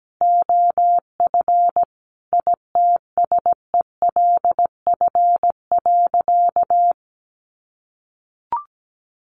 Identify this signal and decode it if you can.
Morse code, heard as a keyed tone, about 17 wpm, 700 Hz. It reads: OF ITSELF.
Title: Sentences from Top 1500 Words 17wpm